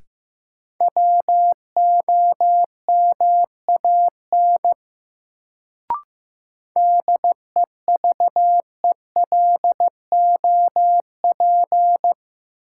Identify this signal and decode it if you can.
Morse code, heard as a keyed tone, about 16 wpm, 700 Hz. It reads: WOMAN DEVELOP
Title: Sentences from Top 1000 Words - Review Part 2 - New Word Drill - 15wpm